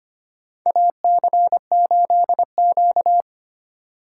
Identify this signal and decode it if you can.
Morse code, heard as a keyed tone, about 25 wpm, 700 Hz. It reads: AC8Q
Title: USA Call Signs - 6x Character Spacing - 25wpm